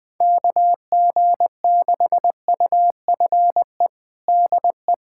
Morse code, heard as a keyed tone, about 21 wpm, 700 Hz. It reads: KG6UFE DE
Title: USA Callsign DE Callsign 20wpm